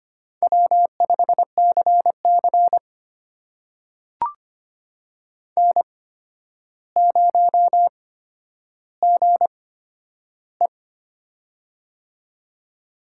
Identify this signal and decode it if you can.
Morse code, heard as a keyed tone, about 25 wpm, spaced as for 11 wpm, 700 Hz. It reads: W5CC N0GE
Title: USA Call Signs - 4 Characters - 8x Character Spacing - 25wpm